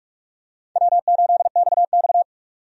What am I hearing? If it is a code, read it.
Morse code, heard as a keyed tone, about 41 wpm, 700 Hz. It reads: W8XX
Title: USA Call Signs - 4 Characters - 45wpm